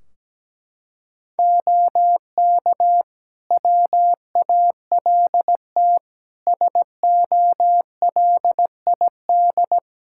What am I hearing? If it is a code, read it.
Morse code, heard as a keyed tone, about 17 wpm, 700 Hz. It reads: OK WALT SOLID
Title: Practice QSO - Exchange 3 - More Info - 17wpm